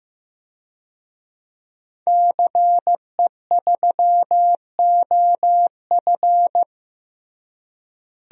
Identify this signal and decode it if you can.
Morse code, heard as a keyed tone, about 15 wpm, 700 Hz. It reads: CE3OF
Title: DXCC-100 Call Signs - 5 Characters - Repeated 2x - 15wpm